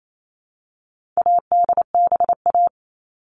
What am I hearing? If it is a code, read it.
Morse code, heard as a keyed tone, about 28 wpm, 700 Hz. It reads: ADBA
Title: All Cut Numbers - 4-Digit 28wpm